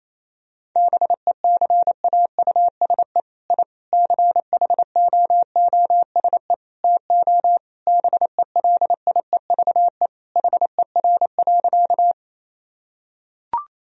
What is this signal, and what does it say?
Morse code, heard as a keyed tone, about 28 wpm, 700 Hz. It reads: BECAUSE I CHOOSE TO BELIEVE HER.
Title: Sentences from Top 1000 Words - Review Part 4 - 28wpm